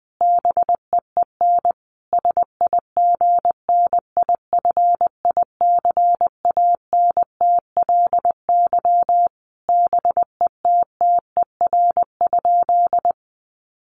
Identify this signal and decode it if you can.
Morse code, heard as a keyed tone, about 20 wpm, 700 Hz. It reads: BEEN SIGNIFICANTLY BETTER?
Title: Sentences from Top 1500 Words 20wpm